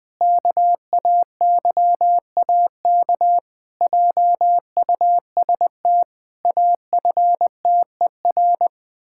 Morse code, heard as a keyed tone, about 20 wpm, 700 Hz. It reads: KAYAK JUST AFTER
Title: Pangram Challenge v2 - 20wpm